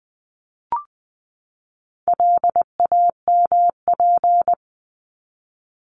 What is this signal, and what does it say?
Morse code, heard as a keyed tone, about 20 wpm, 700 Hz. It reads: LAMP
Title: 260 Lesson 42e - Words 20-20wpm